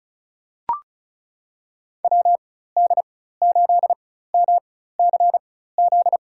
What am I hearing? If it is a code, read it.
Morse code, heard as a keyed tone, about 34 wpm, spaced as for 16 wpm, 700 Hz. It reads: WD8MCZ
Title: USA Call Signs - 6 Characters - 4x Character Spacing - 35wpm